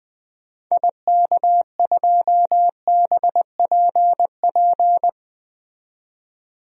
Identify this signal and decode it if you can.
Morse code, heard as a keyed tone, about 20 wpm, 700 Hz. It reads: IK2BPP
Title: DXCC-100 Call Signs - 6 Characters - 20wpm